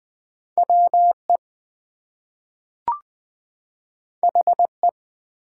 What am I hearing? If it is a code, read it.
Morse code, heard as a keyed tone, about 20 wpm, 700 Hz. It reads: WE HE
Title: Two Letter Words - 20wpm